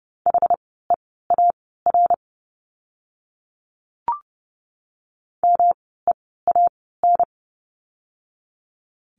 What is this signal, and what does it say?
Morse code, heard as a keyed tone, about 30 wpm, spaced as for 17 wpm, 700 Hz. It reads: HEAR MEAN
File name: course / 30:10wpm / 115 Lesson 19f - Letter w - Words 30wpm